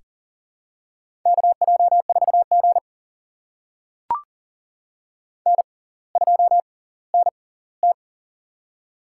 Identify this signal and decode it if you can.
Morse code, heard as a keyed tone, about 39 wpm, spaced as for 20 wpm, 690 Hz. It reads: KJ4G N2NT
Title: USA Call Signs - 4 Characters - 6x Character Spacing - 40wpm